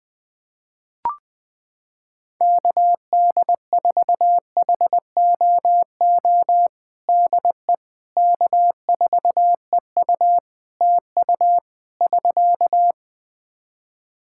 Morse code, heard as a keyed tone, about 21 wpm, 700 Hz. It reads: KD4HOO DE K4EU TU <SK>
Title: Practice QSO - Exchange 4 - Ending - 20wpm